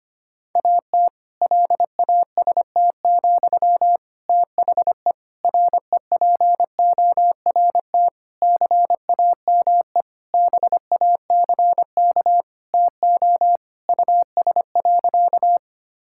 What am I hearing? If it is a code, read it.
Morse code, heard as a keyed tone, about 25 wpm, 700 Hz. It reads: AT LAST, THE REPORT CAME BACK TO US.
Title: Sentences from Top 1000 Words - Review Part 6 - 25wpm